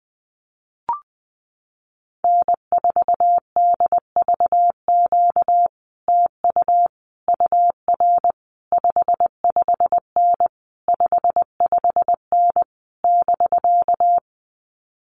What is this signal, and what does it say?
Morse code, heard as a keyed tone, about 20 wpm, 700 Hz. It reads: N4DVQ TU UR 55N 55N <BK>
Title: Common QSO Phrases Encore 20wpm